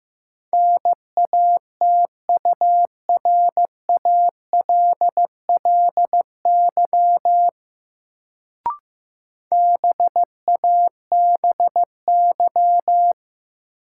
Morse code, heard as a keyed tone, about 15 wpm, 700 Hz. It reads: NATURALLY BABY
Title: Sentences from Top 1300 Words - New Word Drill Alt - 15wpm